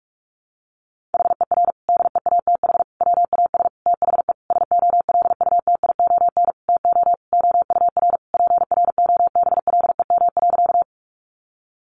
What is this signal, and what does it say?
Morse code, heard as a keyed tone, about 44 wpm, 700 Hz. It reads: HER DEATH WAS THE SOLUTION TO OUR PROBLEM.
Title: Sentences from Top 1300 Words 45wpm